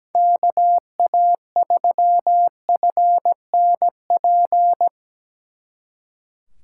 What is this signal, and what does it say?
Morse code, heard as a keyed tone, about 20 wpm, 700 Hz. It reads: KA3FNP
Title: Speed Racing - USA 2x3 Call Signs 17wpm